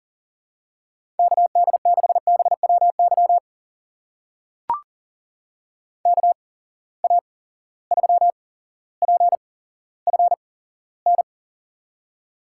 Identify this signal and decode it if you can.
Morse code, heard as a keyed tone, about 40 wpm, spaced as for 17 wpm, 700 Hz. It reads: KD6BWY KA3PFN
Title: USA Call Signs - 6 Characters - 8x Character Spacing - 40wpm